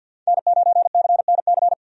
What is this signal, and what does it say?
Morse code, heard as a keyed tone, about 51 wpm, 700 Hz. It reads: N9CNC
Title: USA Call Signs - 5 Characters - 50wpm